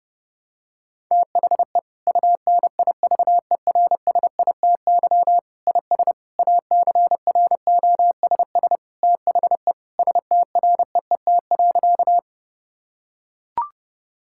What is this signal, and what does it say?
Morse code, heard as a keyed tone, about 30 wpm, 700 Hz. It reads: THE UNIVERSITY IS ACROSS THE STREET.
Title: Sentences from Top 1000 Words - Review Part 5 - 30wpm